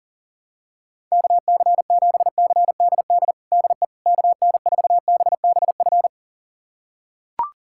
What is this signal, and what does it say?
Morse code, heard as a keyed tone, about 37 wpm, 700 Hz. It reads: KC7CDD DE KN4BBF
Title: USA Callsign DE Callsign - 6 Characters - 40wpm